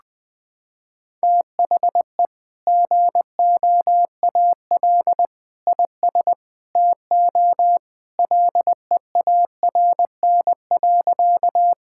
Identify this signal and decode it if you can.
Morse code, heard as a keyed tone, about 21 wpm, 700 Hz. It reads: THE GOAL IS TO LEARN.
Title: Sentences with 2-Word Phrases - 20wpm